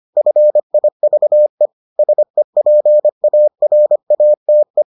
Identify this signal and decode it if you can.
Morse code, heard as a keyed tone, about 25 wpm, 600 Hz. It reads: FIVE SEPARATE